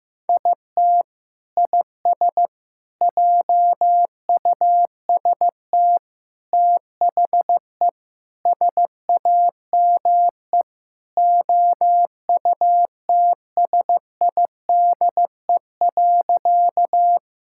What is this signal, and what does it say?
Morse code, heard as a keyed tone, about 16 wpm, 700 Hz. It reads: IT IS JUST THE SAME OUTSIDE.
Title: Sentences from Top 1000 Words - Review Part 6 - 15wpm